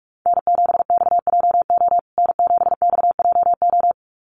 Morse code, heard as a keyed tone, about 44 wpm, 700 Hz. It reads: N7XJO N7XJO
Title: USA Call Signs Repeated 2x 45wpm